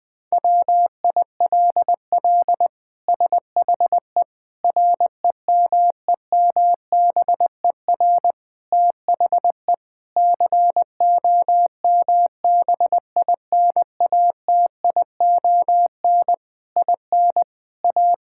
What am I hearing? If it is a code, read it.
Morse code, heard as a keyed tone, about 20 wpm, 700 Hz. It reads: WILL SHE REMEMBER THE COMBINATION IN A
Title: Sentences from Top 1000 Words - Review Part 6 - 20wpm